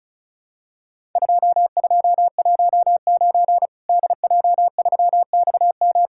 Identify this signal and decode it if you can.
Morse code, heard as a keyed tone, about 34 wpm, 700 Hz. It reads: 2219 DJ3XM
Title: Worked All Europe DX Contest - QTC Exchange - 35wpm